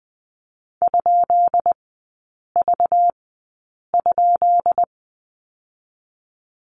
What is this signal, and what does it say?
Morse code, heard as a keyed tone, about 20 wpm, spaced as for 8 wpm, 690 Hz. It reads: ?V?
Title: 191 Lesson 31e - Letter v - Three Characters 20wpm